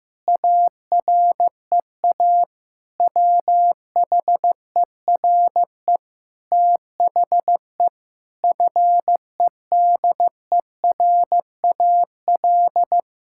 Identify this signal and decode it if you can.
Morse code, heard as a keyed tone, about 16 wpm, 700 Hz. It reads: AREA WHERE THE FEDERAL
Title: Sentences from Top 1000 Words - Review Part 5 - 15wpm